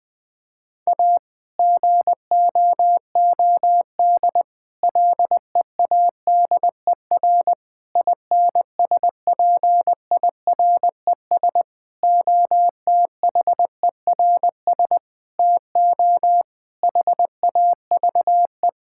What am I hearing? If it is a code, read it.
Morse code, heard as a keyed tone, about 20 wpm, 700 Hz. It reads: A GOOD LEADER INSPIRES OTHERS TO HAVE
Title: Difficult Sentences with Hints - 20wpm